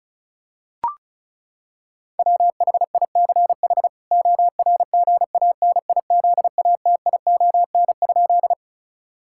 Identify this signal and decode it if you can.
Morse code, heard as a keyed tone, about 34 wpm, 700 Hz. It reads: WHICH ORGANIZATION?
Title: Sentences from Top 1000 Words - Review Part 5 - 35wpm